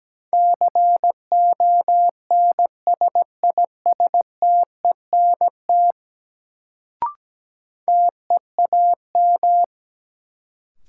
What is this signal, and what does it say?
Morse code, heard as a keyed tone, about 17 wpm, 700 Hz. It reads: CONSISTENT TEAM
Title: Sentences from Top 1500 Words - New Word Drill Alt - 17wpm